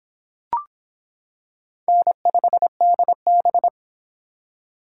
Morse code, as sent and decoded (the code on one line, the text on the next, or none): -. ..... -.. -...
N5DB